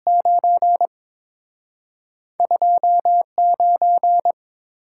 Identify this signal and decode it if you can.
Morse code, heard as a keyed tone, about 30 wpm, spaced as for 26 wpm, 700 Hz. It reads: 9 29
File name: Speed-Racing - Two Letter-Number Combinations - 22wpm